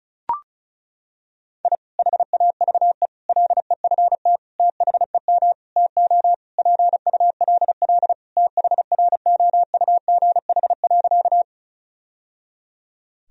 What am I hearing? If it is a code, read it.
Morse code, heard as a keyed tone, about 36 wpm, 700 Hz. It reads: I HAVE LEFT THEM TO PULL THROUGH.
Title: Sentences from Top 1000 Words - Review Part 2 - 35wpm